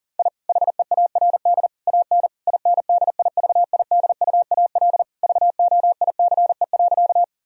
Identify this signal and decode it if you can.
Morse code, heard as a keyed tone, about 40 wpm, 700 Hz. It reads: I HEARD AN INDIVIDUAL VOICE.
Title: Sentences from Top 1000 Words - Review Part 4 - 40wpm